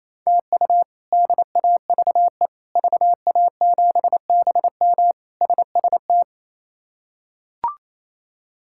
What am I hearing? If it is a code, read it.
Morse code, heard as a keyed tone, about 28 wpm, 700 Hz. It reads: TU DAVE VA7BM SST
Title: K1USN SST QSO Practice - 28wpm